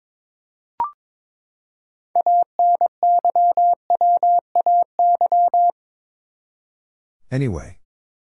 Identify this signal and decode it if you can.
Morse code, heard as a keyed tone, about 21 wpm, 700 Hz. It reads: ANYWAY
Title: Six Letter Words - 22wpm